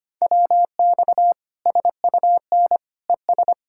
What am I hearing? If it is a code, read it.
Morse code, heard as a keyed tone, about 25 wpm, 700 Hz. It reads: WX SUN ES
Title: Practice QSO - Exchange 2 - Info - 25wpm